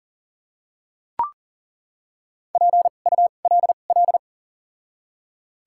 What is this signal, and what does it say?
Morse code, heard as a keyed tone, about 40 wpm, spaced as for 29 wpm, 700 Hz. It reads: PULL
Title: 2x Character Spacing - Top 500 Words - 40wpm